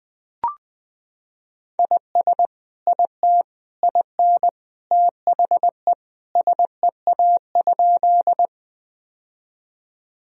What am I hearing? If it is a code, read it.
Morse code, heard as a keyed tone, about 21 wpm, 700 Hz. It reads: IS IT IN THE SEA?
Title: Sentences from Top 1000 Words - Review Part 1 - 20wpm